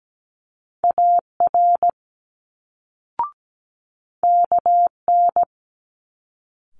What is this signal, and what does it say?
Morse code, heard as a keyed tone, about 17 wpm, 700 Hz. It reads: AR KN
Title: Multi-letter Phonemes Quick Fire 17wpm